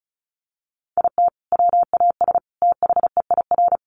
Morse code, heard as a keyed tone, about 34 wpm, 700 Hz. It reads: IT WAS THEIR
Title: Sentences from Top 1200 Words 35wpm